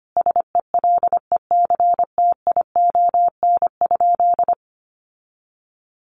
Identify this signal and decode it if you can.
Morse code, heard as a keyed tone, about 25 wpm, 700 Hz. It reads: SELECTION?
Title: Sentences from Top 1400 Words 25wpm